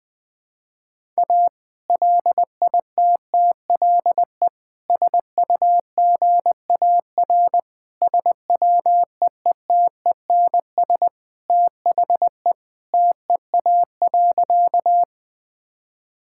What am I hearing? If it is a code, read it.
Morse code, heard as a keyed tone, about 20 wpm, 700 Hz. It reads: A LITTLE SUGAR SWEETENS THE TEA.
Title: Sentences with 2-Word Phrases - 20wpm